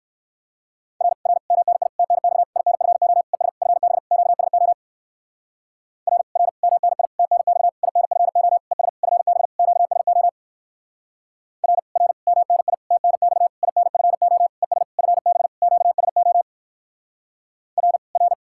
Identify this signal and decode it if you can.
Morse code, heard as a keyed tone, about 65 wpm, 700 Hz. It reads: R R MNI TNX INFO ES FB QSO R R MNI TNX INFO ES FB QSO R R MNI TNX INFO ES FB QSO R R